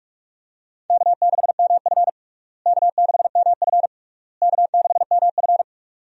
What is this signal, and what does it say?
Morse code, heard as a keyed tone, about 42 wpm, 700 Hz. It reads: K6MF K6MF K6MF
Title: USA Call Signs - 4 Characters - Repeated 3x - 45wpm